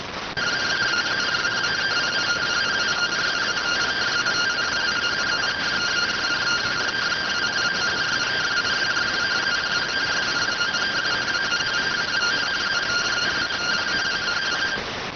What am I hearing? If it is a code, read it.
Two-tone teleprinter signal, about 170 Hz shift.